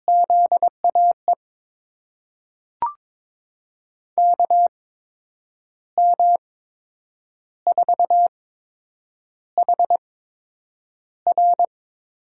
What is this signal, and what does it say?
Morse code, heard as a keyed tone, about 22 wpm, spaced as for 8 wpm, 700 Hz. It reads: ZAE KM4HR
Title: USA Call Signs - 6 Characters - 8x Character Spacing - 22wpm